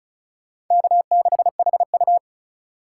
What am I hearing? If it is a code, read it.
Morse code, heard as a keyed tone, about 46 wpm, 700 Hz. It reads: K6HU